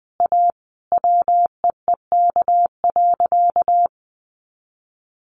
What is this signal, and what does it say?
Morse code, heard as a keyed tone, about 20 wpm, 700 Hz. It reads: A WEEK.
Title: Easy Phrases 20wpm